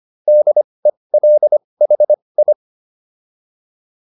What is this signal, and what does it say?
Morse code, heard as a keyed tone, about 25 wpm, spaced as for 20 wpm, 600 Hz. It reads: DELHI